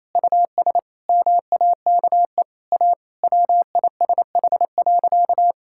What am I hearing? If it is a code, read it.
Morse code, heard as a keyed tone, about 28 wpm, 700 Hz. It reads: US MAKE A WISH.
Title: Sentences from Top 1600 Words 28wpm